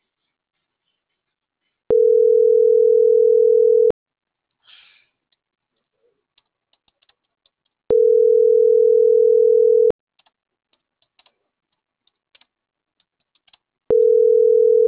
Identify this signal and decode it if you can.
Ringback tone (North America), 440 + 480 Hz.